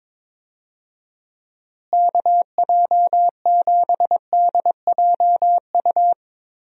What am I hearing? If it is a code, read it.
Morse code, heard as a keyed tone, about 22 wpm, 700 Hz. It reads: KJ7DJU
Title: USA Call Signs - 6 Characters - Repeated 3x - 22wpm